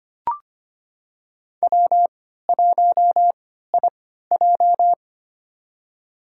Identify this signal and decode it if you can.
Morse code, heard as a keyed tone, about 25 wpm, spaced as for 14 wpm, 700 Hz. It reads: W1IJ
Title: USA Call Signs - 4 Characters - 3x Character Spacing - 25wpm